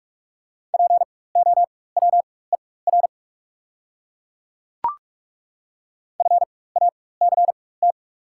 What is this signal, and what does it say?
Morse code, heard as a keyed tone, about 42 wpm, spaced as for 21 wpm, 700 Hz. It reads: POWER FACT
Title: 4x Character Spacing - Top 500 Words - 45wpm